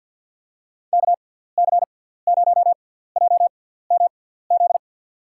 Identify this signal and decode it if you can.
Morse code, heard as a keyed tone, about 53 wpm, spaced as for 17 wpm, 700 Hz. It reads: KC0JMZ
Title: USA Call Signs - 6 Characters - 6x Character Spacing - 50wpm